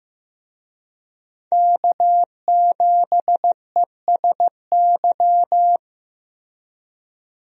Morse code, heard as a keyed tone, about 15 wpm, 700 Hz. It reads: K7ESY
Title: USA Call Signs - 5 Characters - Repeated 2x - 15wpm